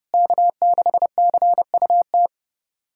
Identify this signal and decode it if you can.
Morse code, heard as a keyed tone, about 32 wpm, 700 Hz. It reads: K6CUT
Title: USA Callsign DE Callsign - 5 Characters - 30wpm